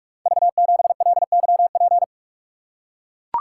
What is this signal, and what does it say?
Morse code, heard as a keyed tone, about 44 wpm, 690 Hz. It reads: V7LYP
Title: Code Groups - 5 Letter:Numbers - 45wpm